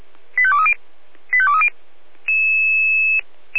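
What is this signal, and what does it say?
Five-tone selective call (ZVEI), heard as quick runs of five tones, tones of about 80 ms. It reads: 85230 85230
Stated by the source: ZVEI five-tone call received by SDR.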